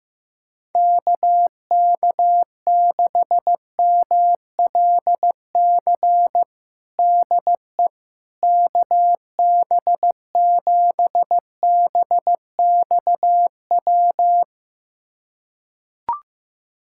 Morse code, heard as a keyed tone, about 15 wpm, 700 Hz. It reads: KK6MLC DE KB7BXW
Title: USA Callsign DE Callsign 15wpm